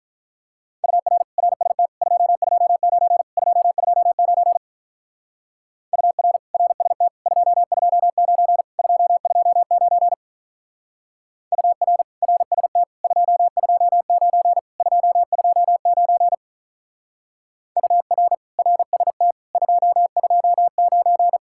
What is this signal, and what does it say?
Morse code, heard as a keyed tone, about 44 wpm, 700 Hz. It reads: UR RST 229 229 UR RST 229 229 UR RST 229 229 UR RST 229
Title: Speed-Racing Common QSO Phrases Encore - 35wpm